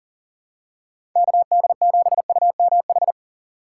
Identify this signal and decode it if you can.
Morse code, heard as a keyed tone, about 40 wpm, 700 Hz. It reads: KD7UMH
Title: USA Call Signs - 6 Characters - 6x Character Spacing - 40wpm